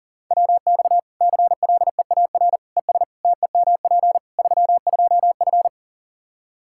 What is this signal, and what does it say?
Morse code, heard as a keyed tone, about 40 wpm, 700 Hz. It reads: WX CLEAR ES TEMP 32F
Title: Practice QSO - Exchange 2 - Info - 40wpm